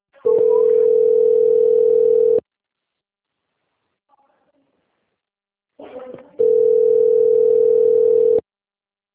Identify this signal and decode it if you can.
Ringback tone (North America), 440 + 480 Hz.
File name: out- - - - - .38368